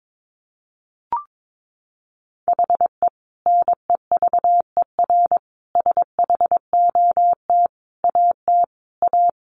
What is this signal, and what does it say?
Morse code, heard as a keyed tone, about 22 wpm, 700 Hz. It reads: HE NEVER SHOT AT A
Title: Sentences from Top 1100 Words 22wpm